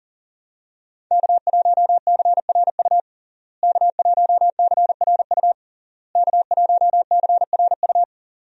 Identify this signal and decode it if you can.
Morse code, heard as a keyed tone, about 41 wpm, 700 Hz. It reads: K1CRU K1CRU K1CRU
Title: USA Call Signs - 5 Characters - Repeated 3x - 40wpm